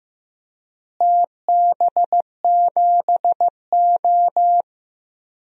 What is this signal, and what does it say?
Morse code, heard as a keyed tone, about 15 wpm, 700 Hz. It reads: TB7O
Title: DXCC-100 Call Signs - 4 Characters - Repeated 2x - 15wpm